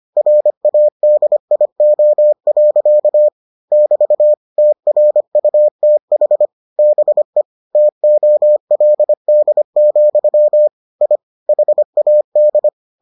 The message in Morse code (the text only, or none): RADIO. <BT> TRUTH BE TOLD, I HAD